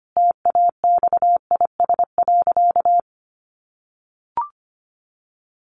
Morse code, heard as a keyed tone, about 25 wpm, 700 Hz. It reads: TAXIS.
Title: Easy Phrases 25wpm